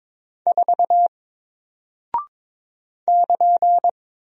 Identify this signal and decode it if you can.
Morse code, heard as a keyed tone, about 22 wpm, 700 Hz. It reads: 4 <KN>
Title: Single Character (Letter:Number:Punctuation:Prosign) Alt 22wpm